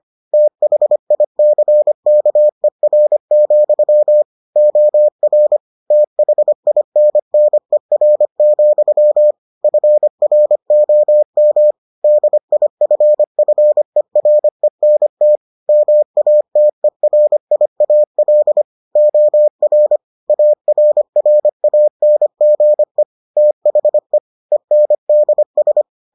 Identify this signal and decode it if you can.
Morse code, heard as a keyed tone, about 25 wpm, 600 Hz. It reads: THICKER, OR THINNER, FROM DIFFERENT MATERIAL OR ARRANGE THE ENDS